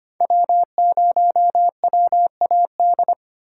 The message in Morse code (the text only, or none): W0WAD